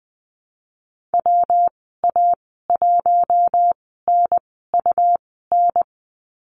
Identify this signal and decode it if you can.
Morse code, heard as a keyed tone, about 20 wpm, spaced as for 15 wpm, 700 Hz. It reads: WA1NUN
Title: 161 Lesson 26g - Letter g - Call Signs 20wpm